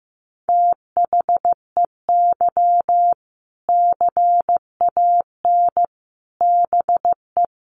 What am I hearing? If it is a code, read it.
Morse code, heard as a keyed tone, about 15 wpm, 700 Hz. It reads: THEY CAN BE
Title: Sentences from Top 1500 Words 15wpm